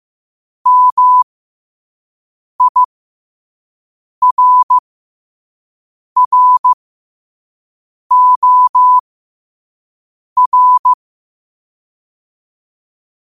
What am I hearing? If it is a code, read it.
Morse code, heard as a keyed tone, about 15 wpm, spaced as for 5 wpm, 1000 Hz. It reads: MIRROR
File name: qotdfiles / 5wpm